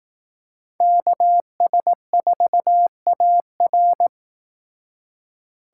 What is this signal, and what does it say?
Morse code, heard as a keyed tone, about 18 wpm, 700 Hz. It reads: KS4AR
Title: Speed Racing - USA 2x2 Call Signs 15wpm